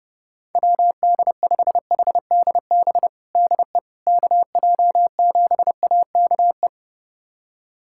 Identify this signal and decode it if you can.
Morse code, heard as a keyed tone, about 30 wpm, 700 Hz. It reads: WD5HDB DE KJ7AKE
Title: USA Callsign DE Callsign - 6 Characters - 30wpm